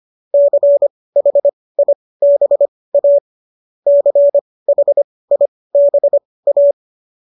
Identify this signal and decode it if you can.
Morse code, heard as a keyed tone, about 25 wpm, spaced as for 18 wpm, 570 Hz. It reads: CHIBA CHIBA